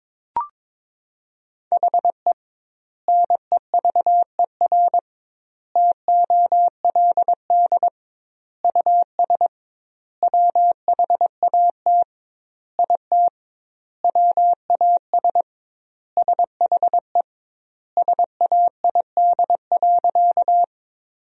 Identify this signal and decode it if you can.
Morse code, heard as a keyed tone, about 23 wpm, 700 Hz. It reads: HE NEVER TOLD US WHAT IT WAS SHE SAID.
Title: 2x Word-Spacing - Sentences from Top 400 Words - 22wpm